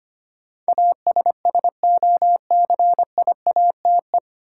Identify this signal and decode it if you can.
Morse code, heard as a keyed tone, about 25 wpm, 700 Hz. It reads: ASSOCIATE
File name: Nine Letter Words - 25wpm